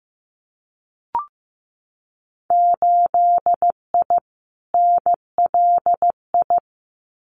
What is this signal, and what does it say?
Morse code, heard as a keyed tone, about 15 wpm, 700 Hz. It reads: 8I NLI
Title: Winter Field Day Exchanges Encore 15wpm